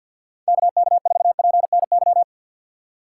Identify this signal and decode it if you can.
Morse code, heard as a keyed tone, about 49 wpm, 690 Hz. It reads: KK4PNY
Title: USA Call Signs - 4x Character Spacing - 50wpm